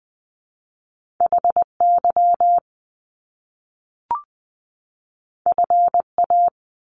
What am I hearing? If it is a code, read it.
Morse code, heard as a keyed tone, about 20 wpm, 700 Hz. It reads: HY FA
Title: Top Two Letter Combinations in Words 20wpm